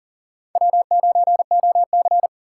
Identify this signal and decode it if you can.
Morse code, heard as a keyed tone, about 40 wpm, 700 Hz. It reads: W9OC
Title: USA Call Signs - 4 Characters - 4x Character Spacing - 40wpm